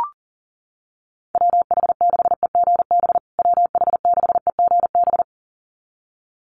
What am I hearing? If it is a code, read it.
Morse code, heard as a keyed tone, about 40 wpm, 700 Hz. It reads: WH6EGB WH6EGB
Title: USA Call Signs Repeated 2x 40wpm